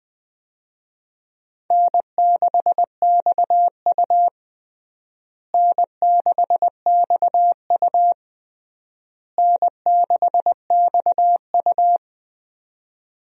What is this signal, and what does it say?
Morse code, heard as a keyed tone, about 20 wpm, 700 Hz. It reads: N6XU N6XU N6XU
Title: USA Call Signs - 4 Characters - Repeated 3x - 20wpm